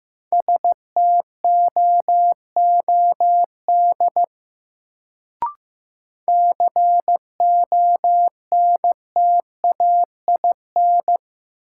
Morse code, heard as a keyed tone, about 15 wpm, 700 Hz. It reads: STOOD CONTAIN
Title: Sentences from Top 500 Words - New Word Drill - 15wpm